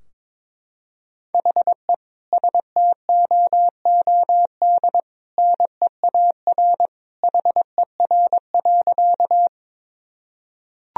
Morse code, heard as a keyed tone, about 22 wpm, 700 Hz. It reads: HE STOOD NEAR HER.
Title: Sentences from Top 1000 Words - Review Part 2 - 22wpm